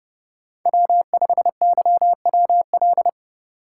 Morse code, heard as a keyed tone, about 35 wpm, 700 Hz. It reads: W5YWL
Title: Speed Racing - USA 1x3 Call Signs 30wpm